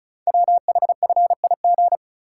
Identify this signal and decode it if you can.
Morse code, heard as a keyed tone, about 36 wpm, 700 Hz. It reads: WHFIG
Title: Code Groups - 5 Letters - 35wpm